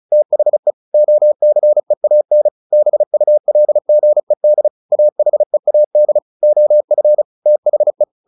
Morse code, heard as a keyed tone, about 34 wpm, 600 Hz. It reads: THE OCEAN BULGED AHEAD OF THE
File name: the old man and the sea / the old man and the sea 1 35wpm